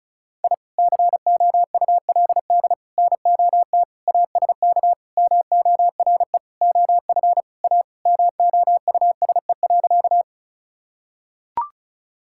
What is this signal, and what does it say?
Morse code, heard as a keyed tone, about 36 wpm, 700 Hz. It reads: I COULD NOT ASK MORE OF A MOUSE.
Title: Sentences from Top 1000 Words - Review Part 5 - 35wpm